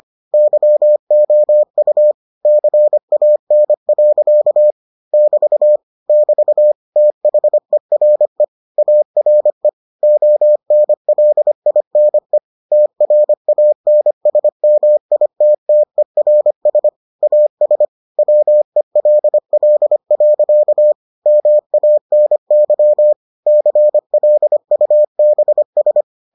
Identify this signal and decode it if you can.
Morse code, heard as a keyed tone, about 25 wpm, 600 Hz. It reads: YOU CAN. <BT> <BT> THERE ARE ONLINE TRANSMITTERS AS WELL. MANY CLUBS